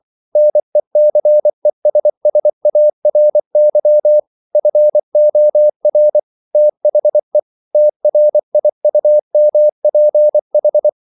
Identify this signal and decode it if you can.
Morse code, heard as a keyed tone, about 24 wpm, 600 Hz. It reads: NECESSARY FOR THE TRIUMPH
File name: CW / Practice / EnglishQuotations0424